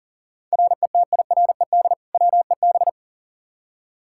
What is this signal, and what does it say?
Morse code, heard as a keyed tone, about 39 wpm, 700 Hz. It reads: RETIRED WEB
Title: K1USN SST QSO Practice 40wpm